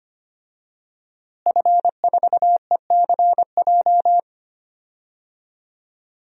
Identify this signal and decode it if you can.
Morse code, heard as a keyed tone, about 25 wpm, 700 Hz. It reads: F4ECJ
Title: DXCC-100 Call Signs - 5 Characters - Repeated 2x - 25wpm